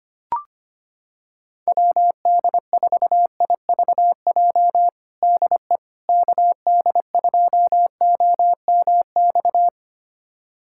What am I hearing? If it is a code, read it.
Morse code, heard as a keyed tone, about 25 wpm, 700 Hz. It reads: WD4IVJ DE KD2OMX
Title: USA Callsign DE Callsign 25wpm